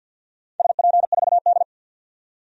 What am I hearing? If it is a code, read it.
Morse code, heard as a keyed tone, about 52 wpm, 700 Hz. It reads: SP4D
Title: DXCC-100 Call Signs - 4 Characters - 50wpm